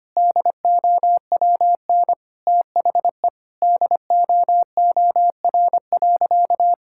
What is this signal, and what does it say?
Morse code, heard as a keyed tone, about 25 wpm, 700 Hz. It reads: DOWN THE DOOR.
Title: Sentences from Top 1000 Words - Review Part 6 - 25wpm